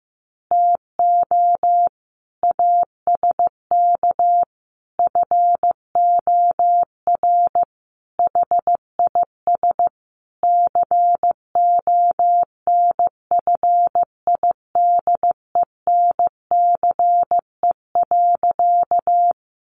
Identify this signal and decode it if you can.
Morse code, heard as a keyed tone, about 14 wpm, 700 Hz. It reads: TO ASK FOR HIS CONFIDENCE.
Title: Sentences from Top 1500 Words 15wpm